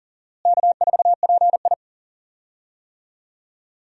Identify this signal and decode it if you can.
Morse code, heard as a keyed tone, about 38 wpm, 700 Hz. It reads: K4PI
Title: USA Call Signs - 4 Characters - Repeated 2x - 40wpm